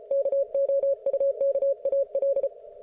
Morse code, heard as a keyed tone, about 34 wpm, 550 Hz. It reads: KOUKAL